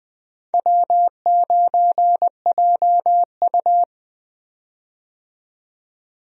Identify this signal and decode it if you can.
Morse code, heard as a keyed tone, about 20 wpm, 700 Hz. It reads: W9JU
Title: USA Call Signs - 4 Characters - 20wpm